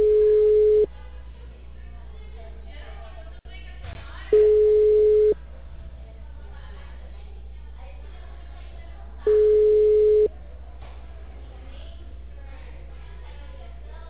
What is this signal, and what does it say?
Ringback tone (Europe, 425 Hz).